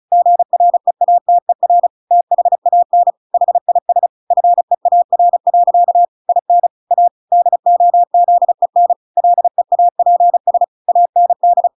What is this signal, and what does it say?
Morse code, heard as a keyed tone, about 35 wpm, 700 Hz. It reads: GREATER THAN HIS FEAR. IN A DOZEN LEAPS AND
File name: Pinocchio / Pinocchio ch6 35 35 ik1hge